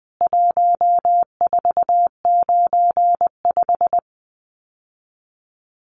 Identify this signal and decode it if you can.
Morse code, heard as a keyed tone, about 20 wpm, 700 Hz. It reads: 1495
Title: CWT Exchange 20wpm